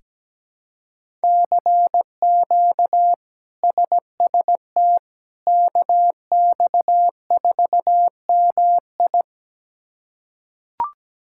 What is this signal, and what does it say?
Morse code, heard as a keyed tone, about 17 wpm, 700 Hz. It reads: CQ SST KX4MI
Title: K1USN SST QSO Practice - 17wpm